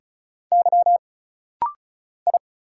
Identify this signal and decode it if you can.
Morse code, heard as a keyed tone, about 33 wpm, 700 Hz. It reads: Y I
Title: Single Letters - Rapid-Fire 35wpm